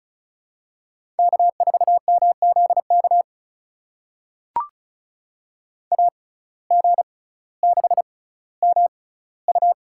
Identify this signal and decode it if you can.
Morse code, heard as a keyed tone, about 36 wpm, spaced as for 18 wpm, 700 Hz. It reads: K4MZK AG6MU
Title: USA Call Signs - 5 Characters - 6x Character Spacing - 35wpm